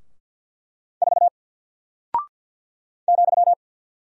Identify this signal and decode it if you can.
Morse code, heard as a keyed tone, about 48 wpm, 700 Hz. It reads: 4 ,
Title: Single Character (Letter:Number:Punctuation:Prosign) Alt 50wpm